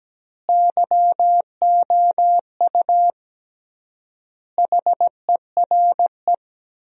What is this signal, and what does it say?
Morse code, heard as a keyed tone, about 17 wpm, 700 Hz. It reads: YOU HERE
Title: 3x Word-Spacing - Sentences from Top 200 Words - 17wpm